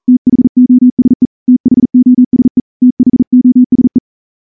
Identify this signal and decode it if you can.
Morse code, heard as a keyed tone, about 38 wpm, 250 Hz. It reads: THOSE THOSE THOSE
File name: morse / morse-training-40WPM